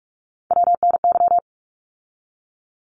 Morse code, heard as a keyed tone, about 44 wpm, 700 Hz. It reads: WNY
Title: ARRL Sections 45wpm